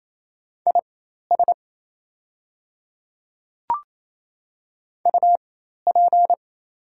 Morse code, heard as a keyed tone, about 27 wpm, spaced as for 13 wpm, 690 Hz. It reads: IS UP
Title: 4x Character Spacing - Top 100 Words - 28wpm